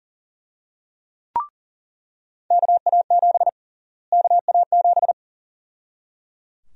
Morse code, heard as a keyed tone, about 37 wpm, 700 Hz. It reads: KA7 KA7
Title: USA Call Sign Prefixes - Repeated 2x - 40wpm